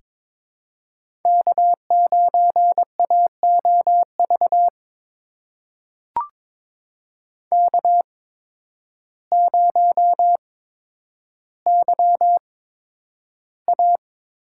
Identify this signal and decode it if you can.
Morse code, heard as a keyed tone, about 22 wpm, spaced as for 10 wpm, 700 Hz. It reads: K9AOV K0YA
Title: USA Call Signs - 5 Characters - 8x Character Spacing - 22wpm